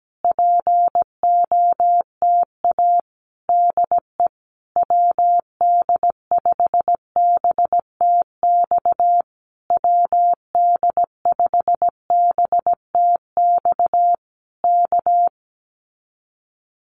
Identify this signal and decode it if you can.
Morse code, heard as a keyed tone, about 17 wpm, 700 Hz. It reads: POTA DE WD5BTX WD5BTX K
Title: Common QSO Phrases Encore 17wpm